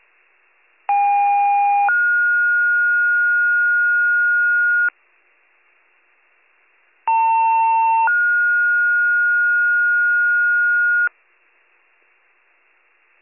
Two-tone page, heard as a one-second tone, then a three-second tone, about 810 then 1430 Hz, about 900 then 1430 Hz. It is heard from 0.9 to 11.1 s.